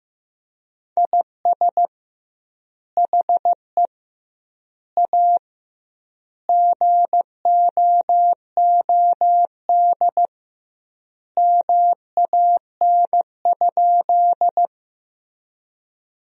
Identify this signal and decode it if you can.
Morse code, heard as a keyed tone, about 15 wpm, 700 Hz. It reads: IS HE A GOOD MAN?
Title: 2x Word-Spacing - Sentences from Top 200 Words - 15wpm